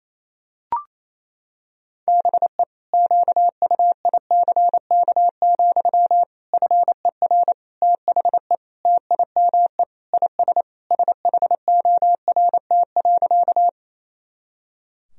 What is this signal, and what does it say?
Morse code, heard as a keyed tone, about 28 wpm, 700 Hz. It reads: BE QUICK, FER THE TIME IS SHORT.
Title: Sentences from Top 500 Words and Abbreviations 28wpm